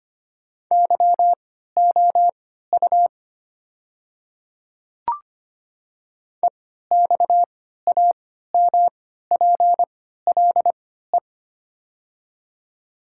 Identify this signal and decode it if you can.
Morse code, heard as a keyed tone, about 25 wpm, spaced as for 14 wpm, 700 Hz. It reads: YOU EXAMPLE
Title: 3x Character Spacing - Top 300 Words - 25wpm